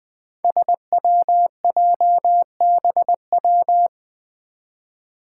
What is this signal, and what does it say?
Morse code, heard as a keyed tone, about 21 wpm, 700 Hz. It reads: SWJBW
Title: Five Challenging Letter-Number Combinations 20wpm